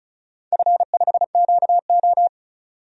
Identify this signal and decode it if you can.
Morse code, heard as a keyed tone, about 36 wpm, 690 Hz. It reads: F5QO
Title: DXCC-100 Call Signs - 4 Characters - 35wpm